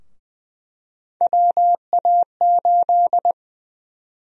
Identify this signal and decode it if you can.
Morse code, heard as a keyed tone, about 20 wpm, 700 Hz. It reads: WA8
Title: USA Call Sign Prefixes - 20wpm